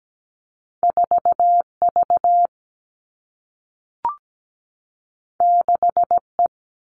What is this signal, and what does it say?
Morse code, heard as a keyed tone, about 17 wpm, 700 Hz. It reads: 4 V 6 E
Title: Two Challenging Letter-Number Combinations 17wpm